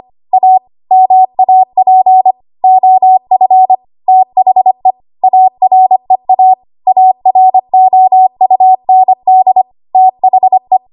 Morse code, heard as a keyed tone, about 25 wpm, 750 Hz. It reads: A MAP OF THE AREA AROUND THE
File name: Morse / 25WPM